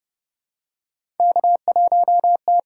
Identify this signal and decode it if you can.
Morse code, heard as a keyed tone, about 29 wpm, 700 Hz. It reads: K1T
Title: USA Call Signs - 5 Characters - 8x Character Spacing - 30wpm